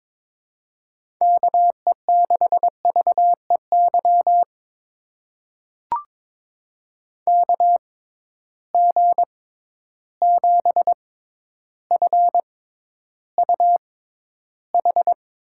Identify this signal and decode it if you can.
Morse code, heard as a keyed tone, about 21 wpm, spaced as for 11 wpm, 700 Hz. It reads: KE6VEY KG7FUH
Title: USA Call Signs - 6 Characters - 6x Character Spacing - 22wpm